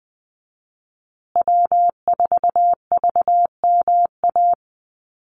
Morse code, heard as a keyed tone, about 20 wpm, 700 Hz. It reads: W4VMA
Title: USA Call Signs Repeated 2x 20wpm